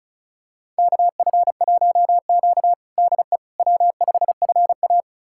Morse code, heard as a keyed tone, about 36 wpm, 690 Hz. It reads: KF1Q DE W5FA
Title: USA Callsign DE Callsign - 4 Characters - 35wpm